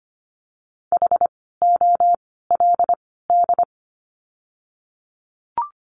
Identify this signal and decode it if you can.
Morse code, heard as a keyed tone, about 25 wpm, 700 Hz. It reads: H O L D
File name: course / 25:10wpm / 094 Lesson 16f - Letter c - Words 25wpm